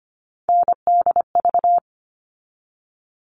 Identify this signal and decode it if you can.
Morse code, heard as a keyed tone, about 25 wpm, 700 Hz. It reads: NDV
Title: All Cut Numbers - 3-Digit 25wpm